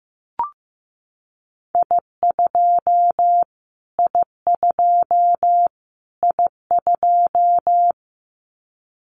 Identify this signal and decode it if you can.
Morse code, heard as a keyed tone, about 15 wpm, 700 Hz. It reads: I2 I2 I2
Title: Callsign Prefixes for Top 25 DXCC - Repeated 3x - 15wpm